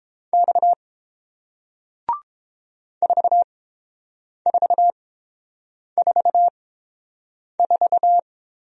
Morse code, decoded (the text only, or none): X 4444